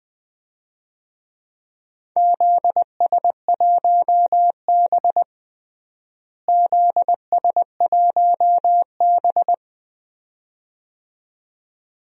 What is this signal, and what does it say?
Morse code, heard as a keyed tone, about 20 wpm, 700 Hz. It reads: ZS1B ZS1B
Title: DXCC-100 Call Signs - 4 Characters - Repeated 2x - 20wpm